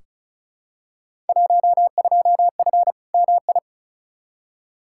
Morse code, heard as a keyed tone, about 36 wpm, 700 Hz. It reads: 12F MI
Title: Field Day Exchanges Encore - 35wpm